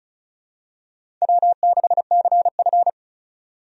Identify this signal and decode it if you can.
Morse code, heard as a keyed tone, about 34 wpm, 700 Hz. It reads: W6CF
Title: USA Call Signs - 4 Characters - 35wpm